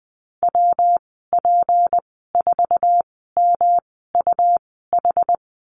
Morse code, heard as a keyed tone, about 20 wpm, spaced as for 15 wpm, 700 Hz. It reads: WP4MUH
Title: 161 Lesson 26g - Letter g - Call Signs 20wpm